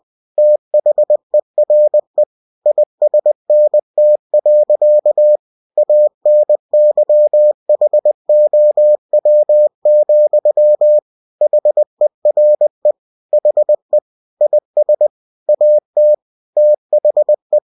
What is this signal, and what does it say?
Morse code, heard as a keyed tone, about 20 wpm, 600 Hz. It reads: THERE ISNT. ANYHOW, HERE HE IS AT THE
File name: cw-books / winnie-the-pooh / winnie-20wpm-page-0000